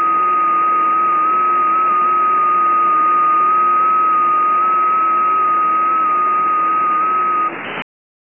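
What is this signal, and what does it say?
FT8.
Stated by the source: FT8 test recording for an FT8 library.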